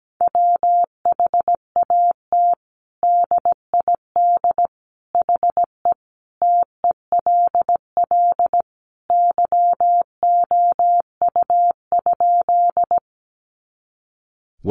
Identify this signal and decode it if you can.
Morse code, heard as a keyed tone, about 17 wpm, 700 Hz. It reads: WHAT DID HE TELL YOU?
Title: Sentences from Top 200 Words 17wpm